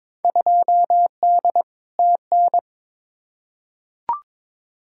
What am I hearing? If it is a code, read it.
Morse code, heard as a keyed tone, about 22 wpm, 700 Hz. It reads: 2D TN
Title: Field Day Exchanges Encore - 22wpm